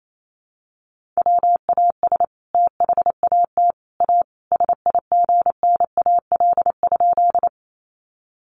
Morse code, heard as a keyed tone, about 28 wpm, 700 Hz. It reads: WAS THAT A SIGNAL?
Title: Sentences from Top 1400 Words 28wpm